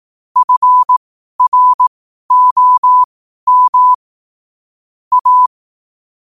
Morse code, heard as a keyed tone, about 18 wpm, spaced as for 12 wpm, 1000 Hz. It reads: FROM A
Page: New quotes every day in morse code at 12 Words per minute.